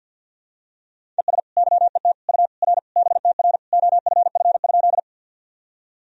Morse code, heard as a keyed tone, about 48 wpm, 700 Hz. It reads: ES YET U R BTR OFF?
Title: Sentences from Top 500 Words and Abbreviations 50wpm